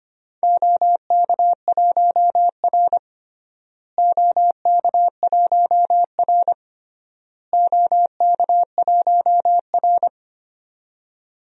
Morse code, heard as a keyed tone, about 25 wpm, 700 Hz. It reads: OK1R OK1R OK1R
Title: DXCC-100 Call Signs - 4 Characters - Repeated 3x - 25wpm